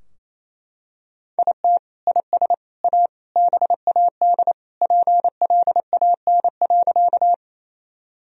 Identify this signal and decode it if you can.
Morse code, heard as a keyed tone, about 28 wpm, 700 Hz. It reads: IT IS A BAD PLAN.
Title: Sentences from Top 1000 Words - Review Part 2 - 28wpm